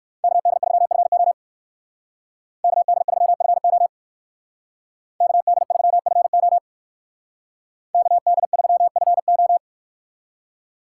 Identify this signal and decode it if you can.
Morse code, heard as a keyed tone, about 58 wpm, 700 Hz. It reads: KD3FO KD3FO KD3FO KD3FO